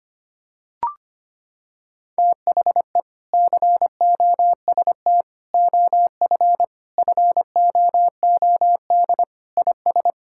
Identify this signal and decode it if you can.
Morse code, heard as a keyed tone, about 25 wpm, 700 Hz. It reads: THE COST OF FOOD IS
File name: practice / 25wpm / Sentences with 3-Word Phrases - 25wpm